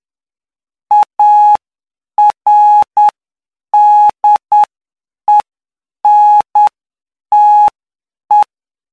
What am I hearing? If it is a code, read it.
Morse code, heard as a keyed tone, about 9 wpm, spaced as for 7 wpm, 840 Hz. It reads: ARDENTE
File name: cw / py2lcd / 12-PY2LCD-Li��o 21